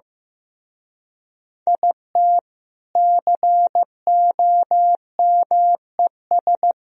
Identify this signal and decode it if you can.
Morse code, heard as a keyed tone, about 17 wpm, 700 Hz. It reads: IT COMES
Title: Speed-Racing - Sets of 3 Words 15wpm